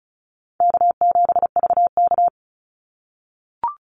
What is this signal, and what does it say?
Morse code, heard as a keyed tone, about 34 wpm, 700 Hz. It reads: K7VK
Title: NA SOTA Activator Callsigns 35wpm